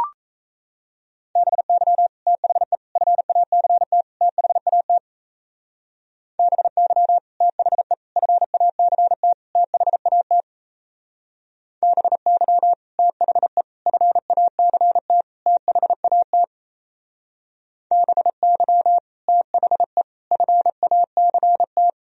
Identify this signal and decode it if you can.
Morse code, heard as a keyed tone, about 33 wpm, 700 Hz. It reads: BY THE FACT THAT BY THE FACT THAT BY THE FACT THAT BY THE FACT
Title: Speed-Racing - Sets of 4 Words 28wpm